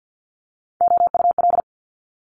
Morse code, heard as a keyed tone, about 50 wpm, 700 Hz. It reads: OUL
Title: Multi-letter Phonemes Quick Fire 50wpm